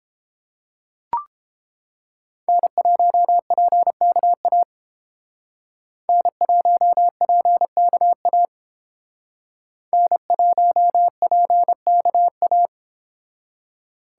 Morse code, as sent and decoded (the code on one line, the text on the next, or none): -. .---- .--. -.- .- / -. .---- .--. -.- .- / -. .---- .--. -.- .-
N1PKA N1PKA N1PKA